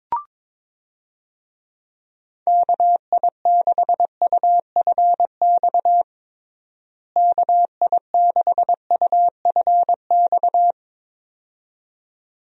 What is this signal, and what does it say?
Morse code, heard as a keyed tone, about 22 wpm, 700 Hz. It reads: KI6UFX KI6UFX
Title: USA Call Signs - 6 Characters - Repeated 2x - 22wpm